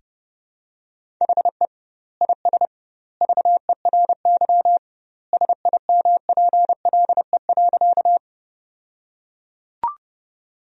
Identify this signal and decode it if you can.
Morse code, heard as a keyed tone, about 30 wpm, 700 Hz. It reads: HE IS VERY SIMPLE.
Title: 2x Word-Spacing - Sentences from Top 400 Words - 30wpm